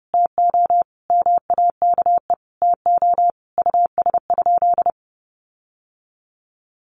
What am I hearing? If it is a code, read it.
Morse code, heard as a keyed tone, about 30 wpm, 700 Hz. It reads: TO MAKE TO US?
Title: Sentences from Top 1400 Words 30wpm